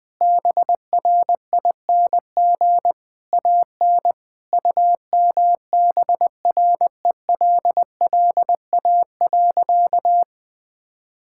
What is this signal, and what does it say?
Morse code, heard as a keyed tone, about 21 wpm, 700 Hz. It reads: BRING AN UMBRELLA.
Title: Sentences with 3-Word Phrases - Spoken Beginning - 20wpm